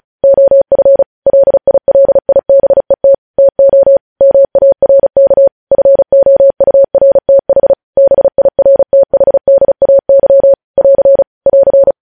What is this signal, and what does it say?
Morse code, heard as a keyed tone, about 36 wpm, 550 Hz. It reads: OF LILIBET TO MARK FOURTH BIRTHDAY <AR> <AR>
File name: podcast / News headlines-1x-35x35